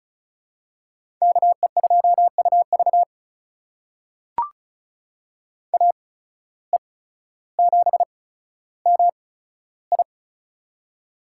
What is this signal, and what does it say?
Morse code, heard as a keyed tone, about 36 wpm, spaced as for 15 wpm, 700 Hz. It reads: KE2UV AE7MI
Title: USA Call Signs - 5 Characters - 8x Character Spacing - 35wpm